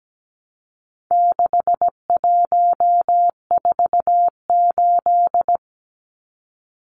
Morse code, heard as a keyed tone, about 17 wpm, 700 Hz. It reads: 6148
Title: Numbers - 4-Digit 17wpm